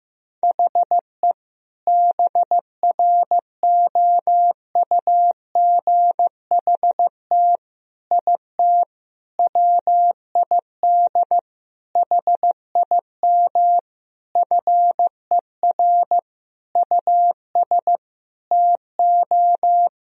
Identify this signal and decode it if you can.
Morse code, heard as a keyed tone, about 16 wpm, 700 Hz. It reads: HE BROUGHT IT WID HIM FER US TO
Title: Sentences from Top 500 Words and Abbreviations 15wpm